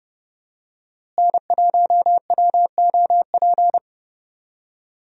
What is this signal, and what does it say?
Morse code, heard as a keyed tone, about 32 wpm, 700 Hz. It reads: N1WOP
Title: USA Call Signs - 5 Characters - Repeated 3x - 30wpm